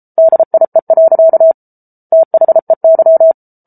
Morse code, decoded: DIE. THEY

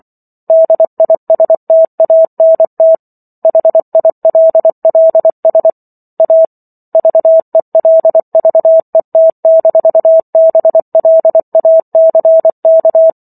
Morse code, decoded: DISTANT HILLS A VELVET-BLACK

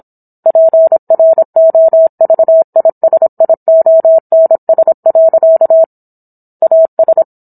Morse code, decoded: PROVISIONS. AS

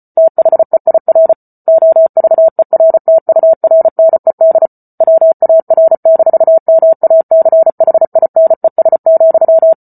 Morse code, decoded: THEIR OVERTURNED WAR-MACHINES,

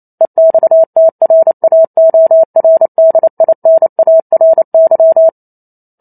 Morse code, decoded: EXTRAORDINARY

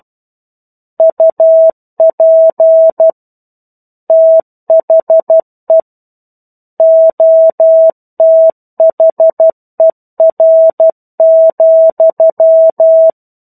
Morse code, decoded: UP THE OTHER,

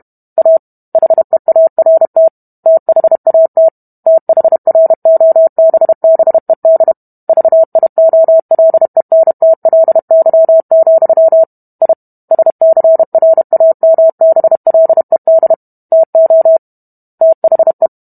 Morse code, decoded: A HEART THAT THROBBED VIOLENTLY, I SCRAMBLED TO THE

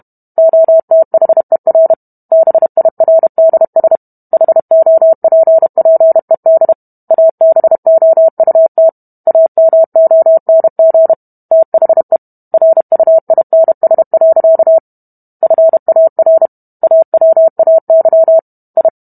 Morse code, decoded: OTHER BIRDS HOPPED ABOUT AMONG THE RUINS. FAR AWAY I

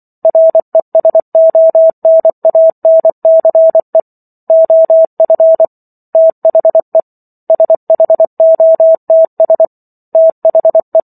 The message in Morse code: RESONANCE OF THE SHOTS THE

